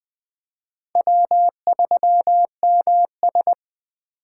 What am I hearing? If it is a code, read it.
Morse code, heard as a keyed tone, about 24 wpm, 700 Hz. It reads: W3MS
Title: Speed Racing - USA 1x2 Call Signs 20wpm